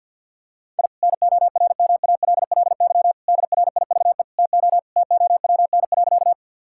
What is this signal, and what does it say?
Morse code, heard as a keyed tone, about 50 wpm, 700 Hz. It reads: I NORMALLY DRIVE TO TOWN.
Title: Sentences from Top 1000 Words - Review Part 6 - 50wpm